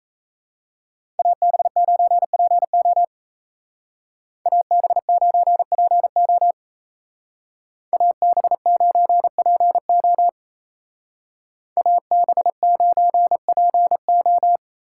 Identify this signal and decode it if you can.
Morse code, heard as a keyed tone, about 35 wpm, 700 Hz. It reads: AB9PO AB9PO AB9PO AB9PO
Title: Speed Racing - USA 2x2 Call Signs 28wpm